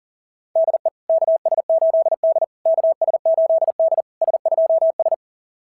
Morse code, heard as a keyed tone, about 42 wpm, 690 Hz. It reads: DE KS8D KS8D S2S
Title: SOTA QSO Practice 40wpm